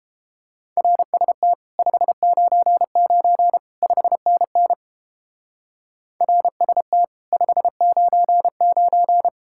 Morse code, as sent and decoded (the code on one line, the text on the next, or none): .-. ... - / ..... ----. ----. / ..... -. -. / .-. ... - / ..... ----. ----.
RST 599 5NN RST 599